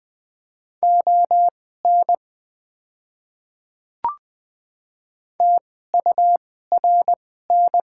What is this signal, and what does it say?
Morse code, heard as a keyed tone, about 20 wpm, spaced as for 14 wpm, 700 Hz. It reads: ON TURN
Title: 2x Character Spacing - Top 300 Words - 20wpm